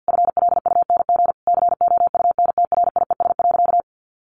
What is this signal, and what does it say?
Morse code, heard as a keyed tone, about 50 wpm, 700 Hz. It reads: FLUNG COUNTRIES.
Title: A Sound Of Thunder 50wpm